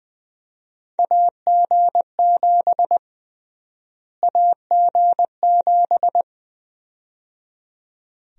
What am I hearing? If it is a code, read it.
Morse code, heard as a keyed tone, about 21 wpm, 700 Hz. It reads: AG7 AG7
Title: USA Call Sign Prefixes - Repeated 2x - 20wpm